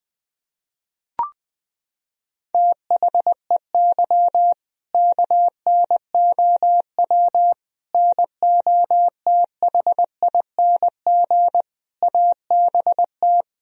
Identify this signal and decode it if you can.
Morse code, heard as a keyed tone, about 20 wpm, 700 Hz. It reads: THEY KNOW NOTHING ABT
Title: Sentences from Top 500 Words and Abbreviations 20wpm